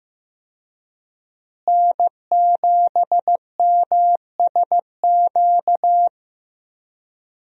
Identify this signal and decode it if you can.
Morse code, heard as a keyed tone, about 15 wpm, 700 Hz. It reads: N7MSQ
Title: USA Call Signs - 5 Characters - Repeated 3x - 15wpm